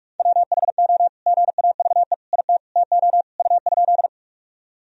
Morse code, heard as a keyed tone, about 45 wpm, 690 Hz. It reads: WHO GAVE IT TO U?
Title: Sentences from Top 500 Words and Abbreviations 45wpm